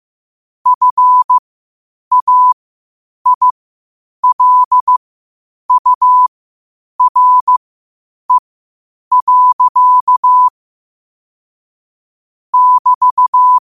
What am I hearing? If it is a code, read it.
Morse code, heard as a keyed tone, about 15 wpm, spaced as for 8 wpm, 1000 Hz. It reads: FAILURE. <BT>
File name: qotdfiles / 8wpm